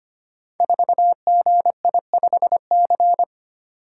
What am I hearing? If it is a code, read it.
Morse code, heard as a keyed tone, about 25 wpm, 700 Hz. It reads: 4GI5C
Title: Code Groups - 5 Letter:Numbers - 25wpm